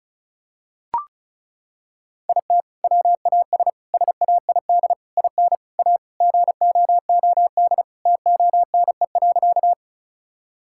Morse code, heard as a keyed tone, about 36 wpm, 700 Hz. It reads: IT WAS SAID IN A GOOD TONE.
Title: Sentences from Top 1600 Words 35wpm